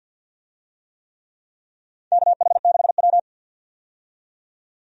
Morse code, heard as a keyed tone, about 48 wpm, 700 Hz. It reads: KH6W
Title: DXCC-100 Call Signs - 4 Characters - 50wpm